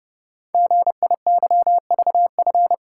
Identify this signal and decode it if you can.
Morse code, heard as a keyed tone, about 29 wpm, 700 Hz. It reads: GIYVF
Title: Code Groups - 5 Letters - 30wpm